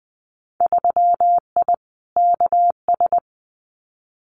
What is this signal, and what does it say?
Morse code, heard as a keyed tone, about 20 wpm, 700 Hz. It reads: 3I KS
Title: Winter Field Day Exchanges 20wpm